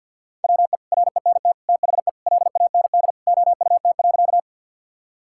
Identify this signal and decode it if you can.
Morse code, heard as a keyed tone, about 52 wpm, 700 Hz. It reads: WE RENT THE LAND OUT.
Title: Sentences from Top 1000 Words - Review Part 5 - 50wpm